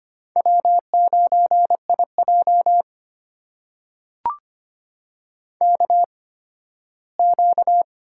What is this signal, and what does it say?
Morse code, heard as a keyed tone, about 25 wpm, spaced as for 15 wpm, 700 Hz. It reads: W9IJ KQ
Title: USA Call Signs - 4 Characters - 8x Character Spacing - 25wpm